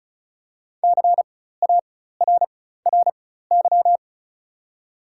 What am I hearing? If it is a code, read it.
Morse code, heard as a keyed tone, about 36 wpm, spaced as for 16 wpm, 700 Hz. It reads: CARRY
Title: 4x Character Spacing - Top 300 Words - 35wpm